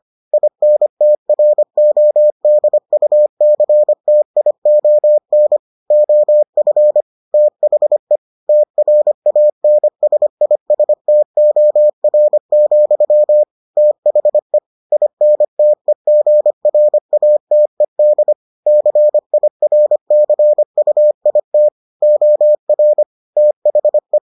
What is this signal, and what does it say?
Morse code, heard as a keyed tone, about 25 wpm, 600 Hz. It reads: INTRODUCTION OF THE TRANSISTOR, THE INTEGRATED CIRCUIT OR THE